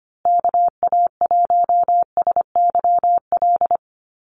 Morse code, heard as a keyed tone, about 25 wpm, 700 Hz. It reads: KA1SYL
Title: USA Call Signs Repeated 3x 25wpm